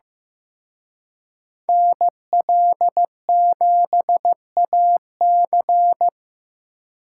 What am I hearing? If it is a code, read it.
Morse code, heard as a keyed tone, about 18 wpm, 700 Hz. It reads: NL7AC
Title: Speed Racing - USA 2x2 Call Signs 15wpm